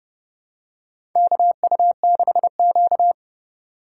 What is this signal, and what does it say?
Morse code, heard as a keyed tone, about 30 wpm, 700 Hz. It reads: KU6Q
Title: USA Call Signs - 4 Characters - 2x Character Spacing - 30wpm